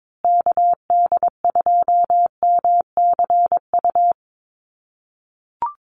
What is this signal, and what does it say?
Morse code, heard as a keyed tone, about 22 wpm, 700 Hz. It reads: KD2MCU
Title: USA Call Signs Repeated 2x 22wpm